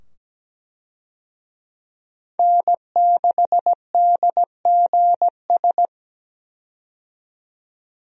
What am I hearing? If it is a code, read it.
Morse code, heard as a keyed tone, about 17 wpm, 700 Hz. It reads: N6DGS
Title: USA Call Signs - 5 Characters - 17wpm